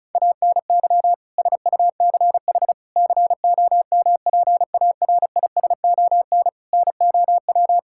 Morse code, heard as a keyed tone, about 36 wpm, 700 Hz. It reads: ANY SUCH COMPARISON NOW
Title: Sentences from Top 1600 Words 35wpm